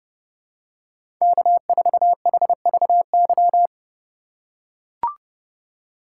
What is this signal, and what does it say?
Morse code, heard as a keyed tone, about 32 wpm, 700 Hz. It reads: K4HVY
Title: USA Call Signs - 5 Characters - 6x Character Spacing - 30wpm